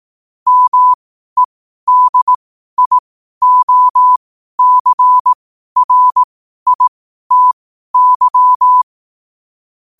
Morse code, heard as a keyed tone, about 18 wpm, spaced as for 12 wpm, 1000 Hz. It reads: MEDIOCRITY